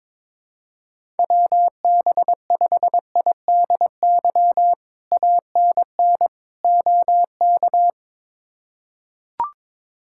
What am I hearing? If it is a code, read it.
Morse code, heard as a keyed tone, about 22 wpm, 700 Hz. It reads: WB5IDY ANN OK
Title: K1USN SST QSO Practice - 22wpm